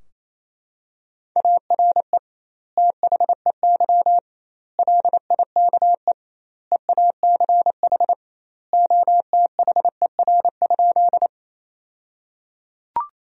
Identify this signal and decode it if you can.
Morse code, heard as a keyed tone, about 28 wpm, 700 Hz. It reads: ARE THEY LIKE EACH OTHER?
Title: 2x Word-Spacing - Sentences from Top 200 Words - 28wpm